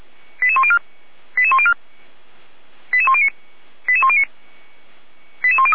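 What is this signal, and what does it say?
Five-tone selective call (ZVEI), heard as quick runs of five tones, tones of about 70 ms. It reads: 88104 88104 88109 88109 88103
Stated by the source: ZVEI five-tone call 88102 received by SDR, archived by audio.f85360.feuerwehr-orlamuende.org fire brigade.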